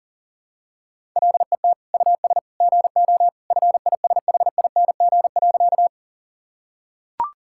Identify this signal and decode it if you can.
Morse code, heard as a keyed tone, about 40 wpm, 700 Hz. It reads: LET US GO FISHING.
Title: Sentences from Top 1000 Words - Review Part 5 - 40wpm